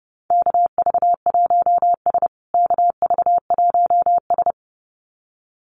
Morse code, heard as a keyed tone, about 30 wpm, 700 Hz. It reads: KV1S KV1S
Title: USA Call Signs Repeated 2x 30wpm